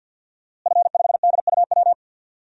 Morse code, heard as a keyed tone, about 46 wpm, 700 Hz. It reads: F5DUW
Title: DXCC-100 Call Signs - 5 Characters - Repeated 2x - 50wpm